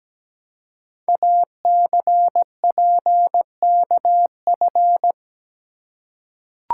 Morse code, heard as a keyed tone, about 17 wpm, 700 Hz. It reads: ACPKF
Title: Code Groups - 5 Letter:Numbers - 17wpm